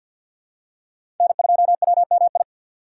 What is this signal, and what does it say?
Morse code, heard as a keyed tone, about 47 wpm, 690 Hz. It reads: N2WMI
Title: USA Call Signs - 5 Characters - 4x Character Spacing - 50wpm